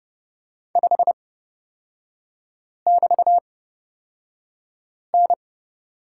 Morse code, heard as a keyed tone, about 30 wpm, spaced as for 5 wpm, 690 Hz. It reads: 5<BT>N
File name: Lesson 37e - Letter x - Three Characters 30wpm